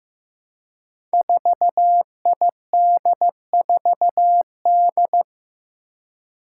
Morse code, heard as a keyed tone, about 15 wpm, 700 Hz. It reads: 4ID4D
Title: Five Challenging Letter-Number Combinations 15wpm